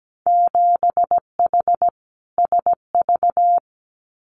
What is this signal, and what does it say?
Morse code, heard as a keyed tone, about 17 wpm, 700 Hz. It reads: 7H SV
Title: Winter Field Day Exchanges Encore 17wpm